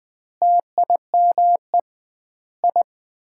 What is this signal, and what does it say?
Morse code, heard as a keyed tone, about 20 wpm, 700 Hz. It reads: TIME I
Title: 2x Word-Spacing - Sets of 4 Words - 20wpm